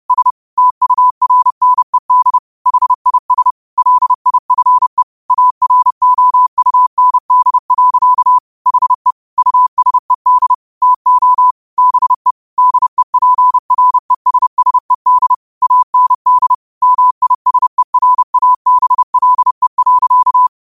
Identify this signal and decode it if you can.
Morse code, heard as a keyed tone, about 30 wpm, 1000 Hz. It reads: S TURNED HIS LIFE AROUND. HE USED TO BE DEPRESSED AND MISERABLE.
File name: qotdfiles / 30wpm